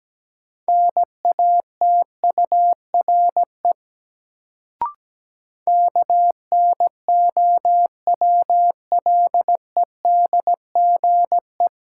Morse code, heard as a keyed tone, about 17 wpm, 700 Hz. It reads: NATURE KNOWLEDGE
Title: Sentences from Top 600 Words - New Word Drill - 17wpm